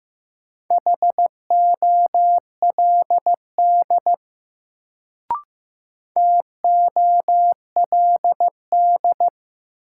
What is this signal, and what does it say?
Morse code, heard as a keyed tone, about 15 wpm, 700 Hz. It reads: HOLD TOLD
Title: Sentences from Top 400 Words - New Word Drill - 15wpm